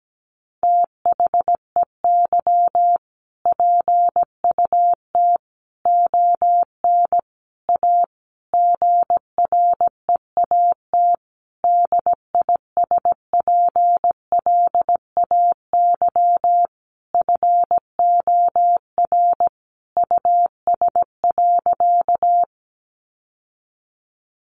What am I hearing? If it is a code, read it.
Morse code, heard as a keyed tone, about 17 wpm, 700 Hz. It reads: THEY PUT ON A GREAT DISPLAY FOR US.
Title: Sentences from Top 1200 Words 17wpm